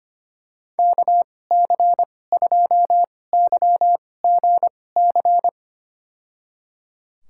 Morse code, heard as a keyed tone, about 25 wpm, spaced as for 18 wpm, 700 Hz. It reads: KC2YGC
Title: USA Call Signs - 6 Characters - 2x Character Spacing - 25wpm